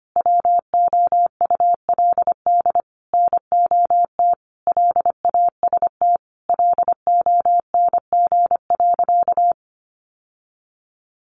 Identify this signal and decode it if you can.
Morse code, heard as a keyed tone, about 25 wpm, 700 Hz. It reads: WOULD NOT LAST LONG.
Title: Sentences from Top 1300 Words 25wpm